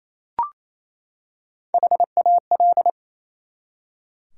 Morse code, heard as a keyed tone, about 27 wpm, 700 Hz. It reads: HAL
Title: USA Common Names - Max 4 Letters - 28wpm